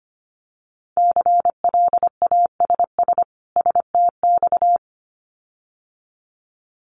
Morse code, heard as a keyed tone, about 25 wpm, 700 Hz. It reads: CLASS STX
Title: Winter Field Day Exchanges 25wpm